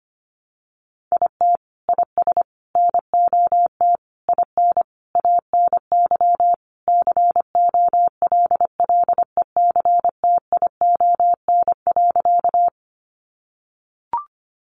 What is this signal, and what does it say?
Morse code, heard as a keyed tone, about 25 wpm, 700 Hz. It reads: IT IS NOT IN ANY COLLECTION.
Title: Sentences from Top 1200 Words 25wpm